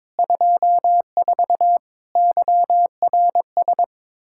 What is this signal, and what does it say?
Morse code, heard as a keyed tone, about 22 wpm, 700 Hz. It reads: 24 YRS
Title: Practice QSO - Exchange 3 - More Info - 22wpm